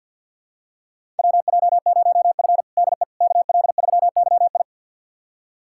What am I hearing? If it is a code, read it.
Morse code, heard as a keyed tone, about 50 wpm, 700 Hz. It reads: WJ0F DE KL3YI
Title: USA Callsign DE Callsign 50wpm